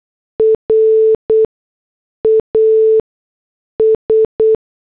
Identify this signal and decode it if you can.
Morse code, heard as a keyed tone, about 8 wpm, spaced as for 6 wpm, 440 Hz. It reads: RAS